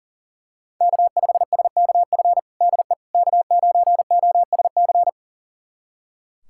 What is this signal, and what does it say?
Morse code, heard as a keyed tone, about 40 wpm, 700 Hz. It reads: K5SKF DE K9OSC
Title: USA Callsign DE Callsign - 5 Characters - 40wpm